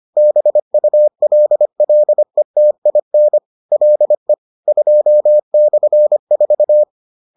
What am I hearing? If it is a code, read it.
Morse code, heard as a keyed tone, about 25 wpm, 600 Hz. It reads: BULLETIN LE 2/4
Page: bulletin UFT sous forme texte et en CW